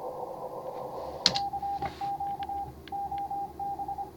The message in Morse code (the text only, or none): CQ OB